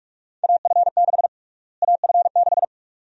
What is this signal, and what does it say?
Morse code, heard as a keyed tone, about 41 wpm, 700 Hz. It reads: AF6 AF6
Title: USA Call Sign Prefixes - Repeated 2x - 45wpm